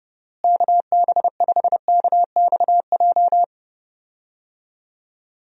Morse code, heard as a keyed tone, about 30 wpm, 700 Hz. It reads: KB5KXJ
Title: USA Call Signs - 6 Characters - Repeated 3x - 30wpm